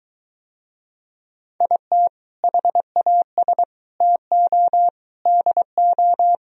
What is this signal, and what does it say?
Morse code, heard as a keyed tone, about 23 wpm, 700 Hz. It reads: IT HAS TO DO